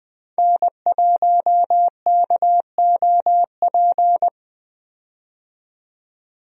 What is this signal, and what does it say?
Morse code, heard as a keyed tone, about 20 wpm, 700 Hz. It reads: N1KOP
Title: USA Call Signs - 5 Characters - 20wpm